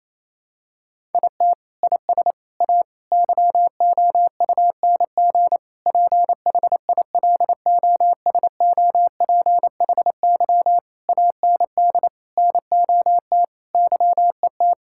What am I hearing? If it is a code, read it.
Morse code, heard as a keyed tone, about 28 wpm, 700 Hz. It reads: IT IS A YOUNG PHILOSOPHY AND NOT YET
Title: Sentences from Top 1000 Words - Review Part 6 - 28wpm